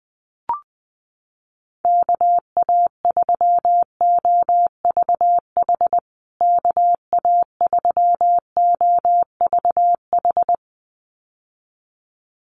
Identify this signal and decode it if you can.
Morse code, heard as a keyed tone, about 19 wpm, 700 Hz. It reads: KA3OVH KA3OVH
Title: USA Call Signs Repeated 2x 20wpm